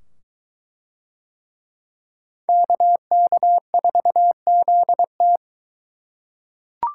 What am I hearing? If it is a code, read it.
Morse code, heard as a keyed tone, about 21 wpm, 700 Hz. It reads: KK4ZT